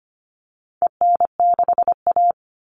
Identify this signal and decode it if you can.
Morse code, heard as a keyed tone, about 25 wpm, 700 Hz. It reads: EN6A
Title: All Cut Numbers - 4-Digit 25wpm